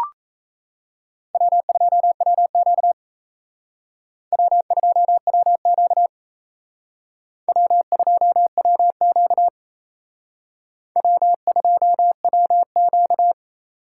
Morse code, heard as a keyed tone, about 36 wpm, 700 Hz. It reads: W2WQ W2WQ W2WQ W2WQ